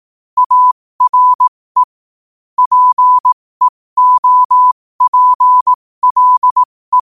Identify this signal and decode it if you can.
Morse code, heard as a keyed tone, about 18 wpm, spaced as for 15 wpm, 1000 Hz. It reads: ARE PEOPLE
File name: qotdfiles / 15wpm